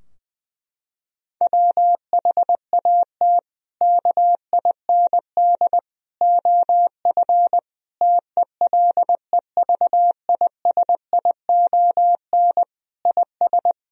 Morse code, heard as a keyed tone, about 20 wpm, 700 Hz. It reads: WHAT KIND OF TELEVISION IS
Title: Sentences from Top 1000 Words - Review Part 4 - 20wpm